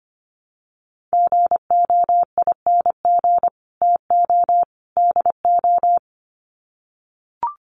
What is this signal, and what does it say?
Morse code, heard as a keyed tone, about 25 wpm, 700 Hz. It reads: GOING TO DO
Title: Sets of 3 Words 25wpm